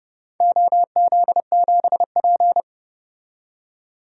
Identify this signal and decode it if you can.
Morse code, heard as a keyed tone, about 30 wpm, 700 Hz. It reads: OZ7P
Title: DXCC-100 Call Signs - 4 Characters - 30wpm